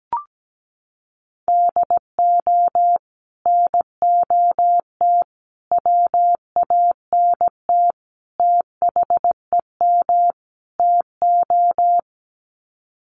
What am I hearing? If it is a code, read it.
Morse code, heard as a keyed tone, about 17 wpm, 690 Hz. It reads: DO NOT WANT THEM TO
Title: Sets of 5 Words Encore 17wpm